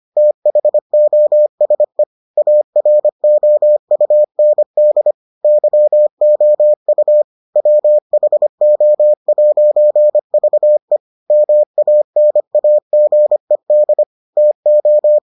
Morse code, heard as a keyed tone, about 25 wpm, 600 Hz. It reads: THOSE AROUND YOU WHO'VE MANAGED TO